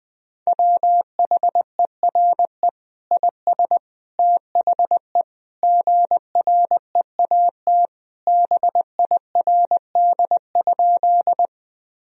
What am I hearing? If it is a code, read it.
Morse code, heard as a keyed tone, about 21 wpm, 700 Hz. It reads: WHERE IS THE GREAT BIRD?
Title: Sentences from Top 1000 Words - Review Part 2 - 20wpm